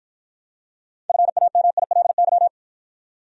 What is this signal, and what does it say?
Morse code, heard as a keyed tone, about 53 wpm, 700 Hz. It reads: FAMILY